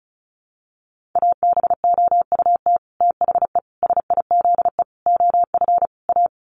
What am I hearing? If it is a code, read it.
Morse code, heard as a keyed tone, about 34 wpm, 700 Hz. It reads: ABOUT THE SIZE OF A
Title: Sets of 5 Words 35wpm